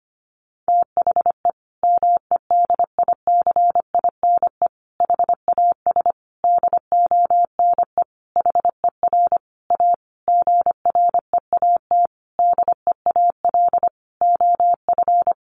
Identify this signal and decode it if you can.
Morse code, heard as a keyed tone, about 25 wpm, 700 Hz. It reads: THE MEDICINE HAS DONE HER A GREAT DEAL OF
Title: Sentences from Top 900 Words 25wpm